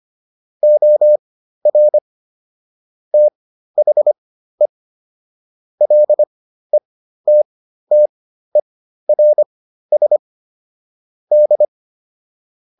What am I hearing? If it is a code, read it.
Morse code, heard as a keyed tone, about 25 wpm, spaced as for 13 wpm, 600 Hz. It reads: OR THE LETTERS D